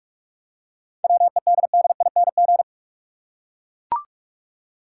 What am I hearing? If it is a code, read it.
Morse code, heard as a keyed tone, about 43 wpm, 700 Hz. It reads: WEDDING
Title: Seven Letter Words - 45wpm